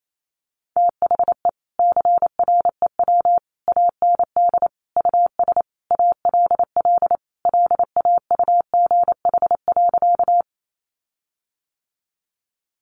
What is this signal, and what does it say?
Morse code, heard as a keyed tone, about 28 wpm, 700 Hz. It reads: THE CREW AND US ALL LAUGH.
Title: Sentences from Top 1500 Words 28wpm